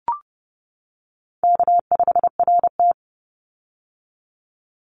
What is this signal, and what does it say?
Morse code, heard as a keyed tone, about 30 wpm, 700 Hz. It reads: K5RT
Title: Call Signs - 30wpm